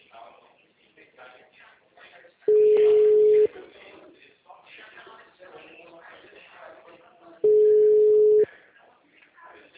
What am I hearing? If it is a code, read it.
Ringback tone (Europe, 425 Hz), with speech.